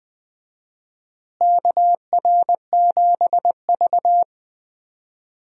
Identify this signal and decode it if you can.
Morse code, heard as a keyed tone, about 20 wpm, 700 Hz. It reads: KR7V
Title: USA Call Signs - 4 Characters - Repeated 2x - 20wpm